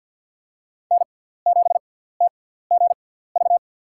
Morse code, heard as a keyed tone, about 48 wpm, spaced as for 17 wpm, 700 Hz. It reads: N7TGV
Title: USA Call Signs - 5 Characters - 6x Character Spacing - 50wpm